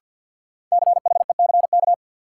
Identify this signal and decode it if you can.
Morse code, heard as a keyed tone, about 53 wpm, 700 Hz. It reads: CHECK
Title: Sentences from Top 1000 Words - Review Part 3 - New Word Drill - 50wpm